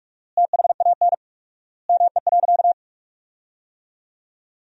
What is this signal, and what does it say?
Morse code, heard as a keyed tone, about 46 wpm, 700 Hz. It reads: THAN ME.
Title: 4x Word-Spacing - Sentences from Top 400 Words - 45wpm